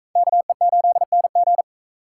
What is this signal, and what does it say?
Morse code, heard as a keyed tone, about 40 wpm, 700 Hz. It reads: KE8NG